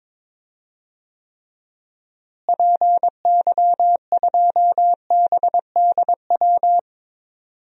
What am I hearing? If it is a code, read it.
Morse code, heard as a keyed tone, about 22 wpm, 700 Hz. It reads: PY2BDW
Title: DXCC-100 Call Signs - 6 Characters - Repeated 3x - 22wpm